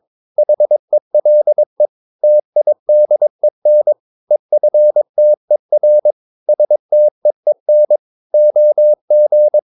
Morse code, heard as a keyed tone, about 22 wpm, 600 Hz. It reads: HELE TIDEN EFTER STEEN OG
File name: Ferie QRS 22wpm